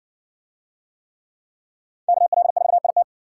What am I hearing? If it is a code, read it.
Morse code, heard as a keyed tone, about 53 wpm, 700 Hz. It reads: KL4IT